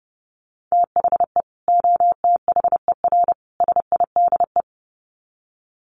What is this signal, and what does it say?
Morse code, heard as a keyed tone, about 30 wpm, 700 Hz. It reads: THE OTHER SIDE
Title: Sets of 3 Words 30wpm